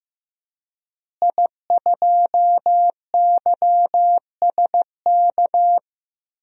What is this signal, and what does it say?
Morse code, heard as a keyed tone, about 15 wpm, 700 Hz. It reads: I2YSK
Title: DXCC-100 Call Signs - 5 Characters - Repeated 2x - 15wpm